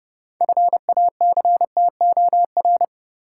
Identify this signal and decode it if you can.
Morse code, heard as a keyed tone, about 29 wpm, 700 Hz. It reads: FACTOR
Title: Sentences from Top 1300 Words - New Word Drill Alt - 30wpm